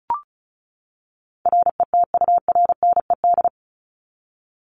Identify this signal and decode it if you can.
Morse code, heard as a keyed tone, about 34 wpm, 700 Hz. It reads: RETURNED
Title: ED Suffix - Top 100 Words with Base Words 35wpm